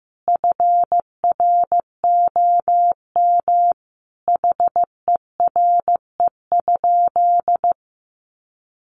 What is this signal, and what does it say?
Morse code, heard as a keyed tone, about 14 wpm, 700 Hz. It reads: FROM HERE?
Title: Sentences from Top 1100 Words 15wpm